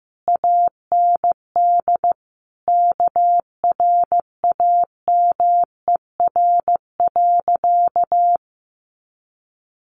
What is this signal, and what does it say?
Morse code, heard as a keyed tone, about 15 wpm, 700 Hz. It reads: AND KRAMER.
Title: A Sound Of Thunder 15wpm